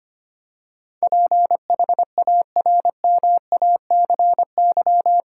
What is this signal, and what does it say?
Morse code, heard as a keyed tone, about 25 wpm, 700 Hz. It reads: PHARMACY
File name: Eight Letter Words - 25wpm